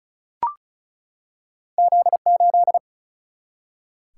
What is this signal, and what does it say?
Morse code, heard as a keyed tone, about 36 wpm, 700 Hz. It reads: Z8
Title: Single-Letter Single-Number 35wpm